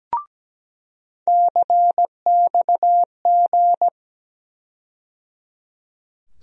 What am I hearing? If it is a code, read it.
Morse code, heard as a keyed tone, about 17 wpm, 700 Hz. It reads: CXG
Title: USA Call Sign Postfixes - 3 Letters 17wpm